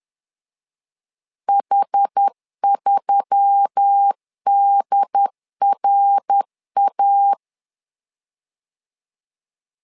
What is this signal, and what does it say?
Morse code, heard as a keyed tone, about 10 wpm, 800 Hz. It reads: H3DRA